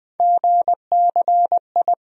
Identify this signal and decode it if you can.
Morse code, heard as a keyed tone, about 21 wpm, 700 Hz. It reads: GCI
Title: USA Call Sign Postfixes - 3 Letters 20wpm